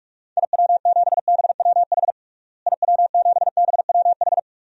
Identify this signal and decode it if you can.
Morse code, heard as a keyed tone, about 42 wpm, 700 Hz. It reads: IW7BWH IW7BWH
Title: DXCC-100 Call Signs - 6 Characters - Repeated 2x - 45wpm